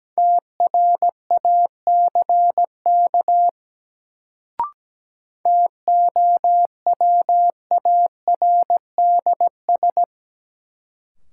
Morse code, heard as a keyed tone, about 17 wpm, 700 Hz. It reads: TRACK TOWARDS
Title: Sentences from Top 1100 Words - New Word Drill Alt - 17wpm